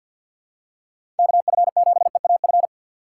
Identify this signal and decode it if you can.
Morse code, heard as a keyed tone, about 51 wpm, 700 Hz. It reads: KF7EAF
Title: USA Call Signs - 6 Characters - 8x Character Spacing - 50wpm